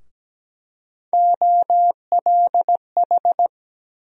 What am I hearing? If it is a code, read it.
Morse code, heard as a keyed tone, about 17 wpm, 700 Hz. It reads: OLH
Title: USA Call Sign Postfixes - 3 Letters 17wpm